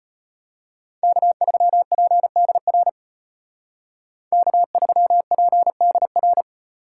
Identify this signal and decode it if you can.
Morse code, heard as a keyed tone, about 33 wpm, 700 Hz. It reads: K3PDR K3PDR